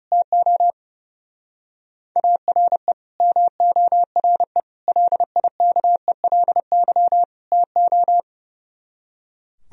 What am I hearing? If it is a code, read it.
Morse code, heard as a keyed tone, about 36 wpm, 700 Hz. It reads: TO ARE MORE LIKELY TO
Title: Speed-Racing - Sets of 4 Words 30wpm